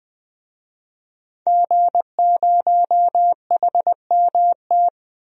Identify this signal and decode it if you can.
Morse code, heard as a keyed tone, about 20 wpm, 700 Hz. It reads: G0HMT
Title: DXCC-100 Call Signs - 5 Characters - 20wpm